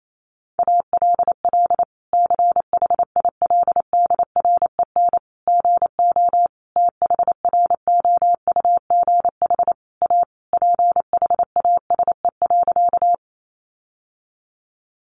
Morse code, heard as a keyed tone, about 28 wpm, 700 Hz. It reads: ALL CHILDREN GO THROUGH A PHASE.
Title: Sentences from Top 1200 Words 28wpm